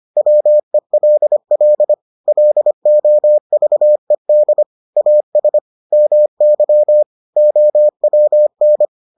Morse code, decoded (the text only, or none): WELL LOVED AS MY OWN